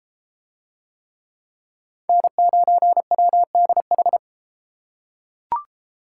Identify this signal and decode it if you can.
Morse code, heard as a keyed tone, about 33 wpm, 700 Hz. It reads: N9WDH